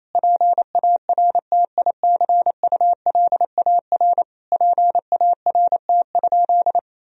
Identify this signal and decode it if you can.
Morse code, heard as a keyed tone, about 28 wpm, 700 Hz. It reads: PARTICULAR PART?
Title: Sentences from Top 1000 Words - Review Part 4 - 28wpm